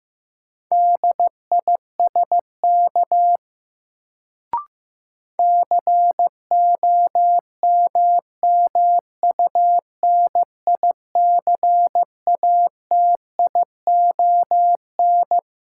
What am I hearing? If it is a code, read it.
Morse code, heard as a keyed tone, about 15 wpm, 700 Hz. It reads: DISK COMMUNICATION
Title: Sentences from Top 1000 Words - Review Part 6 - New Word Drill - 15wpm